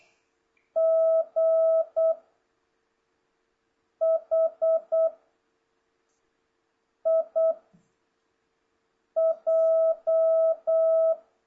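Morse code, heard as a keyed tone, about 8 wpm, 660 Hz. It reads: G H I J
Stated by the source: Morse code lesson from JH8CBH's Zoom Morse training course (ZOOMモールス講習会), session 2 teaching letters G to K.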